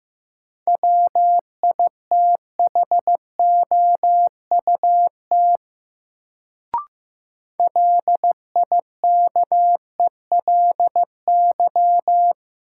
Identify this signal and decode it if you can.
Morse code, heard as a keyed tone, about 15 wpm, 700 Hz. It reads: WITHOUT LIKELY
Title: Sentences from Top 600 Words - New Word Drill - 15wpm